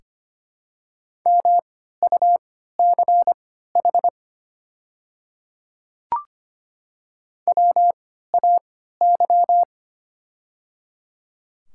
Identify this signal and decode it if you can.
Morse code, heard as a keyed tone, about 25 wpm, spaced as for 14 wpm, 700 Hz. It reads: MUCH WAY
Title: 3x Character Spacing - Top 300 Words - 25wpm